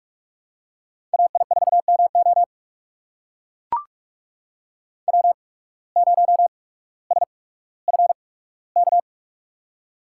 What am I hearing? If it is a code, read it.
Morse code, heard as a keyed tone, about 44 wpm, spaced as for 19 wpm, 700 Hz. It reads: AI4MO W0SFK
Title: USA Call Signs - 5 Characters - 8x Character Spacing - 45wpm